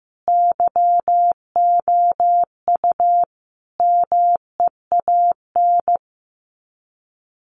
Morse code, heard as a keyed tone, about 15 wpm, 700 Hz. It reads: YOU MEAN
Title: Sets of 4 Words 15wpm